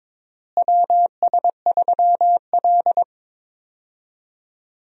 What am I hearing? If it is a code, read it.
Morse code, heard as a keyed tone, about 22 wpm, 700 Hz. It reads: WS3L
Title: USA Call Signs - 4 Characters - Repeated 2x - 22wpm